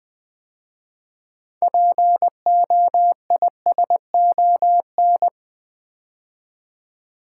Morse code, heard as a keyed tone, about 20 wpm, 700 Hz. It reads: POISON
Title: 6 Letter Words - First 3 Letters Given - 20wpm